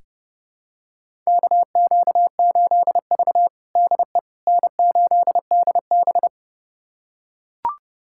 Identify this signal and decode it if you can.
Morse code, heard as a keyed tone, about 30 wpm, 700 Hz. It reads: KQ8V DE N8DB
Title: USA Callsign DE Callsign - 4 Characters - 30wpm